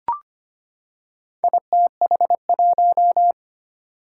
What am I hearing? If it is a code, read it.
Morse code, heard as a keyed tone, about 25 wpm, 700 Hz. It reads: ITH1
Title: Four Challenging Letter-Number Combinations 25wpm